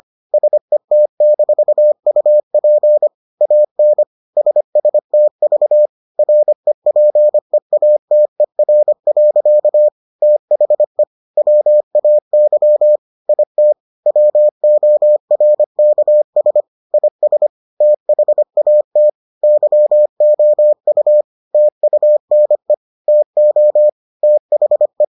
Morse code, heard as a keyed tone, about 25 wpm, 600 Hz. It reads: SET-UP AN SSTV REPEATER. THE WAY IT WORKS IS THAT YOU TUNE TO THE